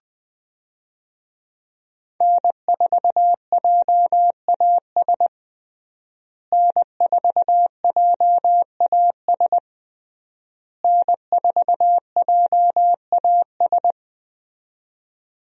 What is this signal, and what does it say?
Morse code, heard as a keyed tone, about 21 wpm, 700 Hz. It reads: N4JAS N4JAS N4JAS
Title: USA Call Signs - 5 Characters - Repeated 3x - 20wpm